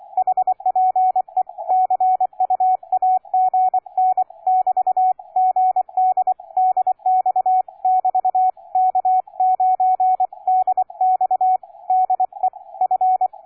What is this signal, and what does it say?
Morse code, heard as a keyed tone, about 24 wpm, 740 Hz, with radio noise. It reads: HPE CUAGN <BT> GD DX <BT> K9DX DE F
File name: cw forum / K9DX 1818 le 17-12-2008 05h47